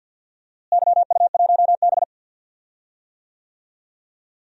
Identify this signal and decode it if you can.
Morse code, heard as a keyed tone, about 52 wpm, 700 Hz. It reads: YU1B
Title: DXCC-100 Call Signs - 4 Characters - Repeated 3x - 50wpm